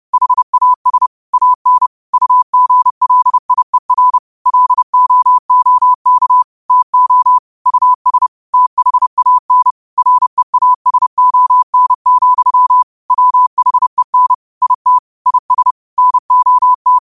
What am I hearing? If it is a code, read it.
Morse code, heard as a keyed tone, about 30 wpm, 1000 Hz. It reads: HAS AN UGLIER LOOK TO US THAN REASON, WHEN IT IS NOT